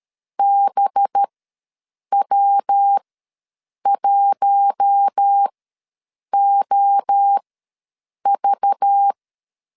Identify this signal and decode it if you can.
Morse code, heard as a keyed tone, about 13 wpm, spaced as for 7 wpm, 800 Hz. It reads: BW1OV
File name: 7 GF9Z ws 12wpm